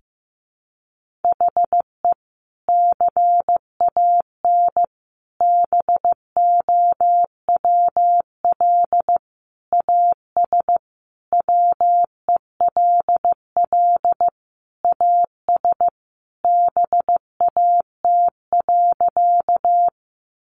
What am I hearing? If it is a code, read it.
Morse code, heard as a keyed tone, about 15 wpm, 700 Hz. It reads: HE CAN BOWL AS WELL AS BAT.
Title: Sentences from Top 1500 Words 15wpm